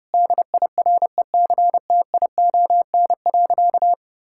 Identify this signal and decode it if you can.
Morse code, heard as a keyed tone, about 30 wpm, 700 Hz. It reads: DIRECTION.
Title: Sentences from Top 1000 Words - Review Part 5 - 30wpm